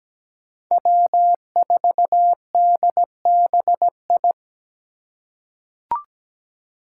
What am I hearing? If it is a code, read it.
Morse code, heard as a keyed tone, about 17 wpm, 700 Hz. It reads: W4DBI
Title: Five Challenging Letter-Number Combinations 17wpm